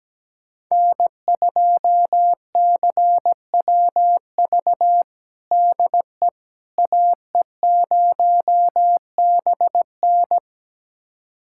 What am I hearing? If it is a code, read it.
Morse code, heard as a keyed tone, about 17 wpm, 700 Hz. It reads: N2CWV DE AE0BN
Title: USA Callsign DE Callsign - 5 Characters - 17wpm